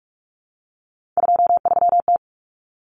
Morse code, heard as a keyed tone, about 44 wpm, 700 Hz. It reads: 23T
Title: Common Cut Numbers - 3-Digit 45wpm